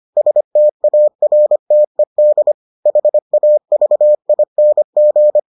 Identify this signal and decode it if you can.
Morse code, heard as a keyed tone, about 25 wpm, 600 Hz. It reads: STARTED HAVING